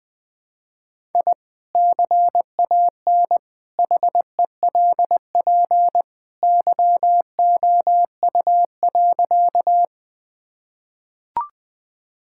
Morse code, heard as a keyed tone, about 20 wpm, 700 Hz. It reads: I CAN HELP YOU.
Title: Sentences with 2-Word Phrases - Spoken Beginning - 20wpm